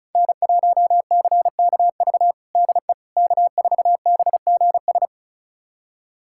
Morse code, heard as a keyed tone, about 36 wpm, 700 Hz. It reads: N1CKV DE K4BGS
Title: USA Callsign DE Callsign - 5 Characters - 35wpm